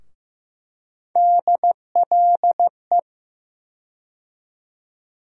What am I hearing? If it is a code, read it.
Morse code, heard as a keyed tone, about 15 wpm, 700 Hz. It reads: DLE
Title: USA Call Sign Postfixes - 3 Letters 15wpm